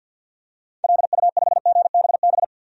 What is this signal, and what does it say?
Morse code, heard as a keyed tone, about 47 wpm, 700 Hz. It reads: LU5GBB
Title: DXCC-100 Call Signs - 6 Characters - 50wpm